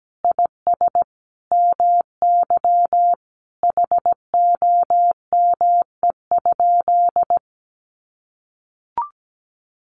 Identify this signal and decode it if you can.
Morse code, heard as a keyed tone, about 17 wpm, 700 Hz. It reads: IS MY HOME?
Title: Sentences from Top 1200 Words 17wpm